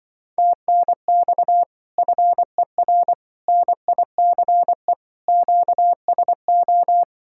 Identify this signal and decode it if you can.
Morse code, heard as a keyed tone, about 24 wpm, 700 Hz. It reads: TNX FER NICE QSO
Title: Speed-Racing Common QSO Phrases - 20wpm